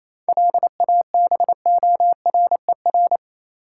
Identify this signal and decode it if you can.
Morse code, heard as a keyed tone, about 28 wpm, 700 Hz. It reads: LABORER
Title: K1USN SST QSO Practice 28wpm